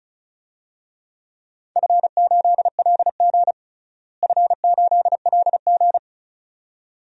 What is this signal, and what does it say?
Morse code, heard as a keyed tone, about 34 wpm, 700 Hz. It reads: F8LG F8LG
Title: DXCC-100 Call Signs - 4 Characters - Repeated 2x - 35wpm